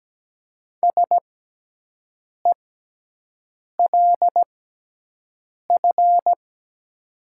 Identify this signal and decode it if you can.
Morse code, heard as a keyed tone, about 17 wpm, spaced as for 6 wpm, 700 Hz. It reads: SELF
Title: 6x Character Spacing - Top 200 Words - 17wpm